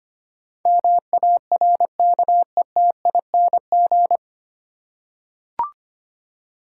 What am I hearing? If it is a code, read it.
Morse code, heard as a keyed tone, about 25 wpm, 700 Hz. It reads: MARKETING
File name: Nine Letter Words - 25wpm